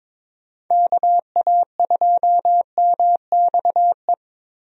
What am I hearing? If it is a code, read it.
Morse code, heard as a keyed tone, about 22 wpm, 700 Hz. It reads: KA2MXE
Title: USA Call Signs - 6 Characters - 22wpm